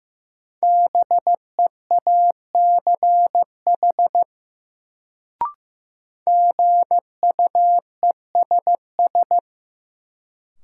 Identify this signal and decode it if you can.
Morse code, heard as a keyed tone, about 15 wpm, 700 Hz. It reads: BEACH GUESS
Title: Sentences from Top 1500 Words - New Word Drill - 15wpm